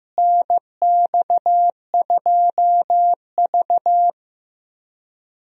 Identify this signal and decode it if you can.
Morse code, heard as a keyed tone, about 15 wpm, 700 Hz. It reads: NX2V
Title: USA Call Signs - 4 Characters - Repeated 3x - 15wpm